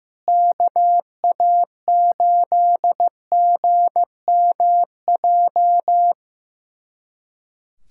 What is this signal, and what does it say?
Morse code, heard as a keyed tone, about 15 wpm, 700 Hz. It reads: KA8GMJ
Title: USA Callsign DE Callsign - 6 Characters - 15wpm